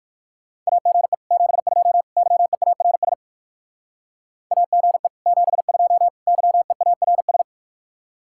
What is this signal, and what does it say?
Morse code, heard as a keyed tone, about 54 wpm, 700 Hz. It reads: AGE 72 YEARS AGE 72 YEARS
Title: Speed-Racing Common QSO Phrases - 45wpm